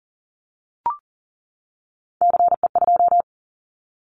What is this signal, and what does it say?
Morse code, heard as a keyed tone, about 39 wpm, 700 Hz. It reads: CE2
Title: Callsign Prefixes for Top 100 DXCC - 40wpm